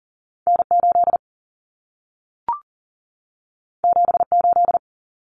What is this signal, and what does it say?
Morse code, heard as a keyed tone, about 40 wpm, 700 Hz. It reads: N8 78
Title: Common Cut Numbers - 2-Digit 40wpm